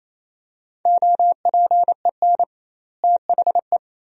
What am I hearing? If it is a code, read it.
Morse code, heard as a keyed tone, about 28 wpm, 700 Hz. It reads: OPEN THE
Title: 2x Word-Spacing - Sentences from Top 300 Words - 28wpm